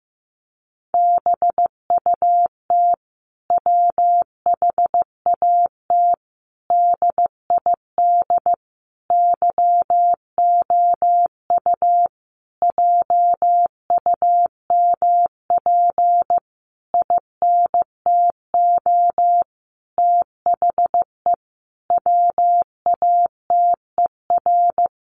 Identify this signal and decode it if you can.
Morse code, heard as a keyed tone, about 14 wpm, 700 Hz. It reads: BUT WHAT DID YOU JUMP INTO THE WATER
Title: Sentences from Top 1300 Words 15wpm